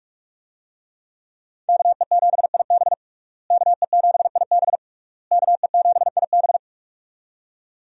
Morse code, heard as a keyed tone, about 46 wpm, 700 Hz. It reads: KE7IB KE7IB KE7IB
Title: USA Call Signs - 5 Characters - Repeated 3x - 45wpm